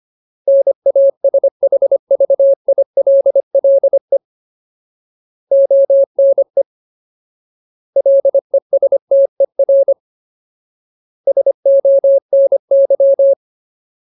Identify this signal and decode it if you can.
Morse code, heard as a keyed tone, about 25 wpm, 550 Hz. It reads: NASHVILLE ONE LESTER SONY
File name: Ultimate Word List / 25WPM Ultimate